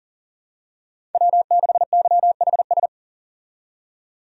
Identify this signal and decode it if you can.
Morse code, heard as a keyed tone, about 40 wpm, 700 Hz. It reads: W6YHS
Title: USA Call Signs - 5 Characters - 40wpm